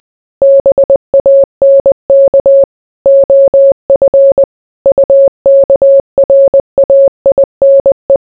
Morse code, heard as a keyed tone, about 20 wpm, 550 Hz. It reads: BANK OF UKRAINE